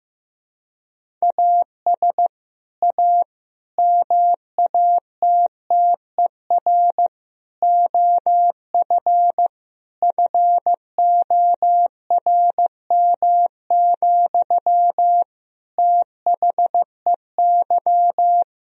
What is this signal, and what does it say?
Morse code, heard as a keyed tone, about 15 wpm, 700 Hz. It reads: AS A MATTER OF FORM, THEY
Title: Sentences from Top 1000 Words - Review Part 1 - 15wpm